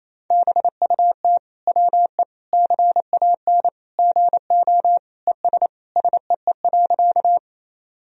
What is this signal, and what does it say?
Morse code, heard as a keyed tone, about 28 wpm, 700 Hz. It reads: BUT WE CAN GO ES SEE.
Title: Sentences from Top 100 Words and Abbreviations 28wpm